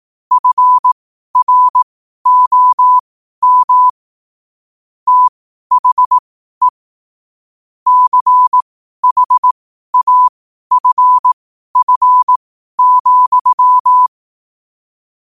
Morse code, heard as a keyed tone, about 18 wpm, spaced as for 12 wpm, 1000 Hz. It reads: FROM THE CHAFF,
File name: qotdfiles / 12wpm